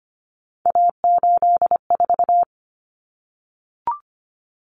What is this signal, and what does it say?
Morse code, heard as a keyed tone, about 25 wpm, 700 Hz. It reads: A84
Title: Common Cut Numbers - 3-Digit 25wpm